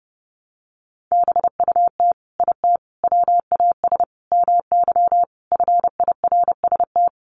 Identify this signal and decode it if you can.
Morse code, heard as a keyed tone, about 30 wpm, 700 Hz. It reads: BUT IT WAS MY FIRST
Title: Sentences from Top 1300 Words 30wpm